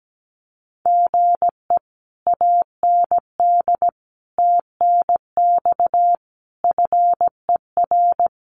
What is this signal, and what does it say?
Morse code, heard as a keyed tone, about 17 wpm, 700 Hz. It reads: GE AND TNX FER
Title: Common QSO Phrases Encore 17wpm